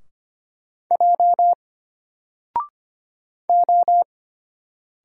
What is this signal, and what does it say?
Morse code, heard as a keyed tone, about 25 wpm, 700 Hz. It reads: J O
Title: Single Letter-Number Alt 25wpm